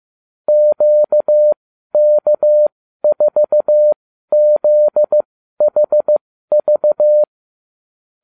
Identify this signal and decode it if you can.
Morse code, heard as a keyed tone, about 15 wpm, spaced as for 12 wpm, 600 Hz. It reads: QK4ZHV
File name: cw / ubacw34 3